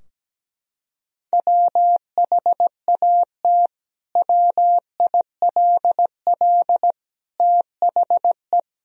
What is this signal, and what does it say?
Morse code, heard as a keyed tone, about 17 wpm, 700 Hz. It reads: WHAT WILL THE
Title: Sentences from Top 1000 Words - Review Part 3 - 17wpm